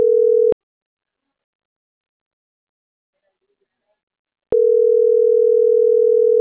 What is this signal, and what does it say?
Ringback tone (North America), 440 + 480 Hz.